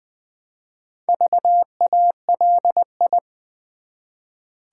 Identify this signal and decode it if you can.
Morse code, heard as a keyed tone, about 20 wpm, 700 Hz. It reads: VALI
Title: Build A Word - 1 Letter At A Time - 5-Letter Words - 20wpm